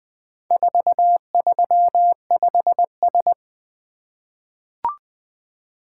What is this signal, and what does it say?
Morse code, heard as a keyed tone, about 20 wpm, 700 Hz. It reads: 435S
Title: Four Letter-Number Combinations - 20wpm